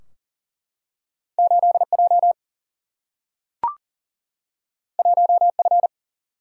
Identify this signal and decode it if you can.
Morse code, heard as a keyed tone, about 42 wpm, 700 Hz. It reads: 8J 1F
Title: Single-Letter Single-Number 40wpm